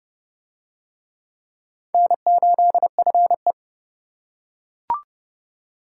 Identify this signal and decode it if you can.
Morse code, heard as a keyed tone, about 27 wpm, 700 Hz. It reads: N8FE